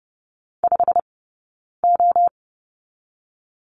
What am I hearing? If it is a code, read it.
Morse code, heard as a keyed tone, about 30 wpm, 700 Hz. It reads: 5 O
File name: course / 30:10wpm / 086 Lesson 15d - Number 5 - Two Characters 30wpm